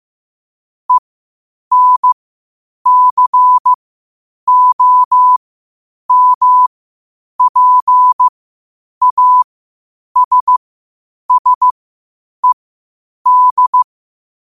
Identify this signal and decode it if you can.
Morse code, heard as a keyed tone, about 15 wpm, spaced as for 8 wpm, 1000 Hz. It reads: ENCOMPASSED
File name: qotdfiles / 8wpm